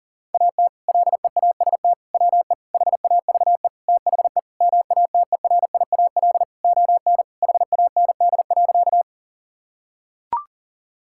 Morse code, heard as a keyed tone, about 40 wpm, 700 Hz. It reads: AT LEAST WE HAVE THE MATERIAL ON HAND.
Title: Sentences from Top 1000 Words - Review Part 5 - 40wpm